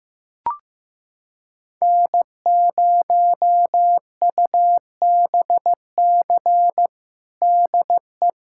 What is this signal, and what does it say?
Morse code, heard as a keyed tone, about 15 wpm, 700 Hz. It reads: N0UBC DE
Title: USA Callsign DE Callsign - 5 Characters - 15wpm